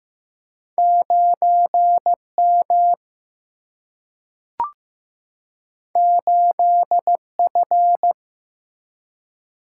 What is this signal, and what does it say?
Morse code, heard as a keyed tone, about 15 wpm, 700 Hz. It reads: 9 M 8 F
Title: Single-Letter Single-Number 15wpm